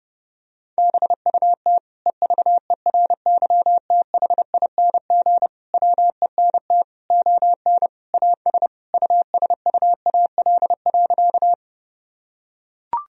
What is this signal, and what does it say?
Morse code, heard as a keyed tone, about 32 wpm, 700 Hz. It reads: BUT EVERYTHING WENT ON AS USUAL.
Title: Sentences from Top 1000 Words - Review Part 2 - 30wpm